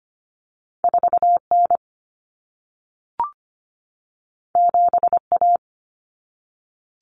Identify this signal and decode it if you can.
Morse code, heard as a keyed tone, about 25 wpm, 700 Hz. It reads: 4N 7A
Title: Common Cut Numbers - 2-Digit 25wpm